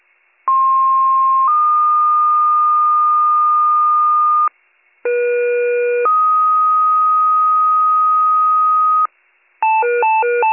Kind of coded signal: two-tone page

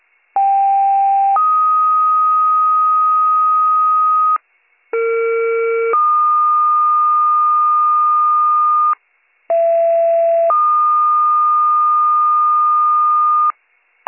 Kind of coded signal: two-tone page

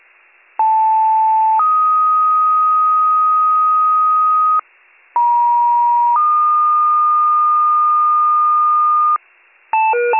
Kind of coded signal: two-tone page

